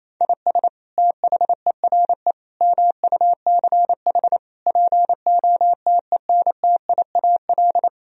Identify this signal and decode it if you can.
Morse code, heard as a keyed tone, about 28 wpm, 700 Hz. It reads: IS THERE MUCH POTENTIAL
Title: Sentences from Top 1000 Words - Review Part 4 - 28wpm